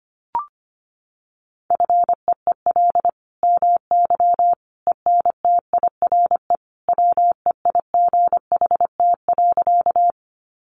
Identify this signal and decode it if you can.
Morse code, heard as a keyed tone, about 25 wpm, 700 Hz. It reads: FEEL MY ENTIRE WEIGHT.
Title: Sentences from Top 800 Words 25wpm